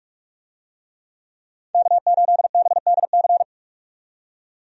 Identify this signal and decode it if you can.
Morse code, heard as a keyed tone, about 46 wpm, 700 Hz. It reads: K8BDC
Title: USA Call Signs - 5 Characters - 45wpm